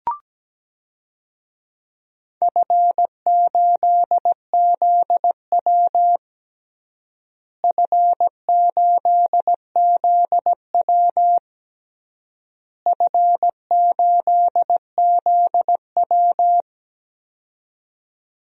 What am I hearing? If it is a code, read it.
Morse code, heard as a keyed tone, about 17 wpm, 700 Hz. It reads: F8ZW F8ZW F8ZW
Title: DXCC-100 Call Signs - 4 Characters - Repeated 3x - 17wpm